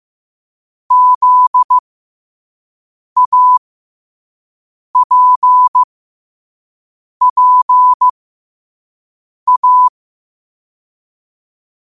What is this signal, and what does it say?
Morse code, heard as a keyed tone, about 15 wpm, spaced as for 5 wpm, 1000 Hz. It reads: ZAPPA
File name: qotdfiles / 5wpm